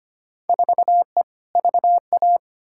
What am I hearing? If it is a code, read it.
Morse code, heard as a keyed tone, about 25 wpm, 700 Hz. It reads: 4E VA
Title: Field Day Exchanges Encore - 25wpm